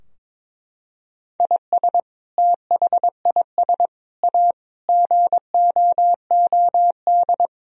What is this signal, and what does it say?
Morse code, heard as a keyed tone, about 22 wpm, 700 Hz. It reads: IS THIS A GOOD
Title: Sentences from Top 1000 Words - Review Part 4 - 22wpm